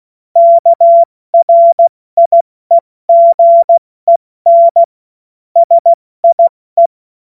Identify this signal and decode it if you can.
Morse code, heard as a keyed tone, about 16 wpm, 680 Hz. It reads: KRIEGEN SIE
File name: Morsekurs / Witze / Witz 29. August 2023 70 F80